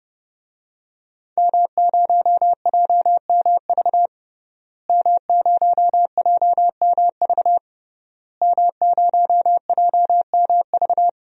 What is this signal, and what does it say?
Morse code, heard as a keyed tone, about 30 wpm, 700 Hz. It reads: M0JMV M0JMV M0JMV
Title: DXCC-100 Call Signs - 5 Characters - Repeated 3x - 30wpm